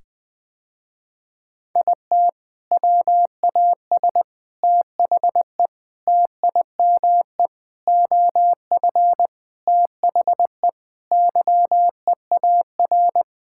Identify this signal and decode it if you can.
Morse code, heard as a keyed tone, about 21 wpm, 700 Hz. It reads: IT WAS THE TIME OF THE YEAR
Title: Sentences with 4-Word Phrases - Spoken Beginning - 20wpm